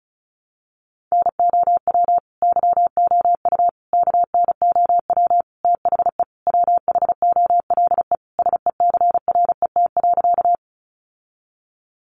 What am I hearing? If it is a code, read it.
Morse code, heard as a keyed tone, about 34 wpm, 700 Hz. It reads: NOW YOU KNOW THE WHOLE SECRET.
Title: Sentences from Top 1200 Words 35wpm